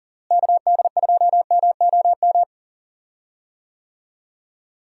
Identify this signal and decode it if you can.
Morse code, heard as a keyed tone, about 39 wpm, 700 Hz. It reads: KD2MOM
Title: USA Call Signs - 6 Characters - Repeated 3x - 40wpm